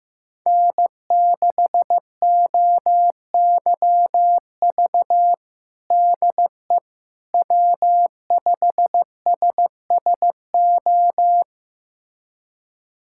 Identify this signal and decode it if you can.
Morse code, heard as a keyed tone, about 15 wpm, 700 Hz. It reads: N6OYV DE W5SSO
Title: USA Callsign DE Callsign - 5 Characters - 15wpm